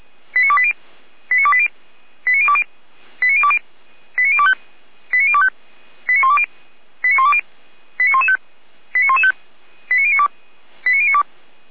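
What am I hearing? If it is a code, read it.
Five-tone selective call (ZVEI), heard as quick runs of five tones, tones of about 70 ms. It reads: 89280 89280 89020 89020 89025 89025 89120 89120 89115 89115 89092 89092
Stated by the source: ZVEI five-tone call 89262 received by SDR, archived on the Feuerwehr Orlamünde fire brigade server.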